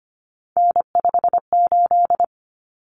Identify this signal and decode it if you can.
Morse code, heard as a keyed tone, about 25 wpm, 700 Hz. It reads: N58
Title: Common Cut Numbers - 3-Digit 25wpm